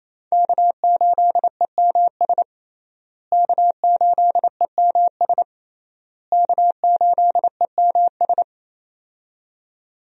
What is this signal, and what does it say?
Morse code, heard as a keyed tone, about 27 wpm, 700 Hz. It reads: K8EMS K8EMS K8EMS
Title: USA Call Signs - 5 Characters - Repeated 3x - 28wpm